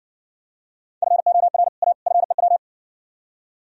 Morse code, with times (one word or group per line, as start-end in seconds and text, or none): 1.0-1.7 FOR
1.8-1.9 A
2.1-2.6 FEW